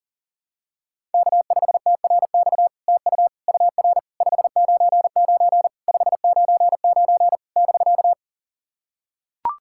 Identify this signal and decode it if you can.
Morse code, heard as a keyed tone, about 44 wpm, 700 Hz. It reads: K5TRX TU UR 599 599 <BK>
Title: SOTA QSO Practice 40wpm